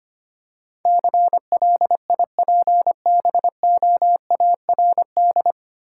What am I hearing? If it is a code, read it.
Morse code, heard as a keyed tone, about 25 wpm, 690 Hz. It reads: CLIPBOARD
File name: practice / 25wpm / Easy Words with Hints - 25wpm